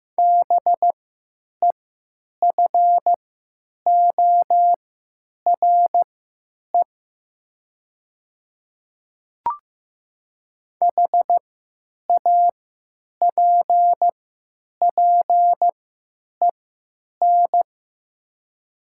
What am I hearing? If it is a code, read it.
Morse code, heard as a keyed tone, about 15 wpm, spaced as for 9 wpm, 700 Hz. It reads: BEFORE HAPPEN
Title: 3x Character Spacing - Top 500 Words - 15wpm